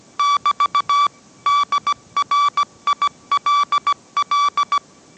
Morse code, heard as a keyed tone, about 17 wpm, 1150 Hz. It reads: <BT> DRILL
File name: mrx / mrx080 18wpm